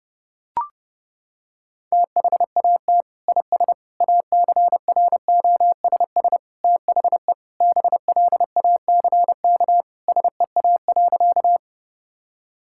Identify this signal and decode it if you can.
Morse code, heard as a keyed tone, about 30 wpm, 700 Hz. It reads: THAT IS ACROSS THE BLACK SEA.
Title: Sentences from Top 1000 Words - Review Part 5 - 30wpm